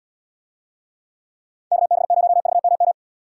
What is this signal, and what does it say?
Morse code, heard as a keyed tone, about 77 wpm, 700 Hz. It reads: KB0VMM